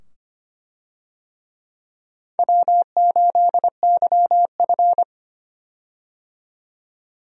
Morse code, heard as a keyed tone, about 25 wpm, 700 Hz. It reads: W8YF
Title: USA Call Signs - 4 Characters - 25wpm